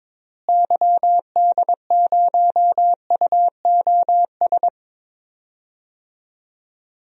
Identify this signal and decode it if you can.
Morse code, heard as a keyed tone, about 22 wpm, 700 Hz. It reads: YD0UOS
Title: DXCC-100 Call Signs - 6 Characters - Repeated 2x - 22wpm